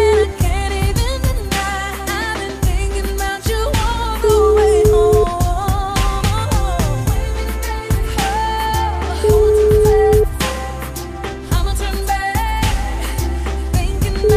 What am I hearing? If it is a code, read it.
Ringback tone (Europe, 425 Hz), with noise.